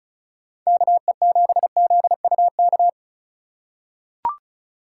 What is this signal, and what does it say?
Morse code, heard as a keyed tone, about 36 wpm, 700 Hz. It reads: KE7ZUK
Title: USA Call Signs - 6 Characters - 4x Character Spacing - 35wpm